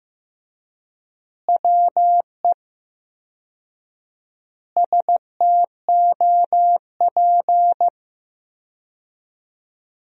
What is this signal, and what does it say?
Morse code, heard as a keyed tone, about 15 wpm, 700 Hz. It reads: WE STOP
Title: 4x Word-Spacing - Sentences from Top 300 Words - 15wpm